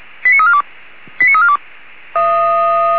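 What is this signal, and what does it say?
Five-tone selective call (ZVEI), heard as quick runs of five tones, tones of about 70 ms. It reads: 89342 89342
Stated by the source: ZVEI five-tone call received by SDR.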